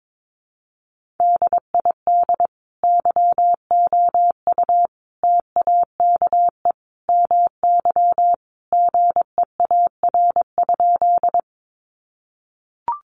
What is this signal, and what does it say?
Morse code, heard as a keyed tone, about 22 wpm, 700 Hz. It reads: DID YOU TAKE MY GEAR?
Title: Sentences from Top 1400 Words 22wpm